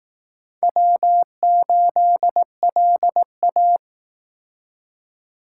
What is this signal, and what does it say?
Morse code, heard as a keyed tone, about 18 wpm, 700 Hz. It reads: W8LA
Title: Speed Racing - USA 1x2 Call Signs 15wpm